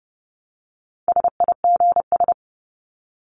Morse code, heard as a keyed tone, about 30 wpm, 700 Hz. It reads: SIGS
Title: Abbreviations 30wpm